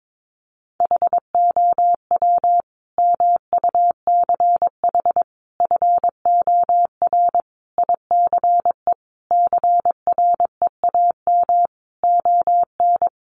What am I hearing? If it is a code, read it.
Morse code, heard as a keyed tone, about 22 wpm, 700 Hz. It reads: HOW MUCH FOR ICE CREAM ON
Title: Sentences from Top 1500 Words 22wpm